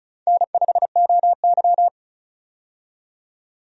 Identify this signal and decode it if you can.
Morse code, heard as a keyed tone, about 34 wpm, 700 Hz. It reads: N5OY
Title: USA Call Signs - 4 Characters - 35wpm